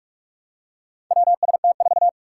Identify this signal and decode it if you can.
Morse code, heard as a keyed tone, about 44 wpm, 700 Hz. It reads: WST4
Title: Four Challenging Letter-Number Combinations 45wpm